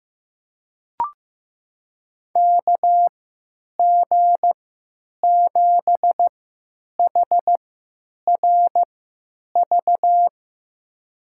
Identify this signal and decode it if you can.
Morse code, heard as a keyed tone, about 15 wpm, spaced as for 9 wpm, 700 Hz. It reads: KG7HRV
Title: USA Call Signs - 3x Character Spacing - 15wpm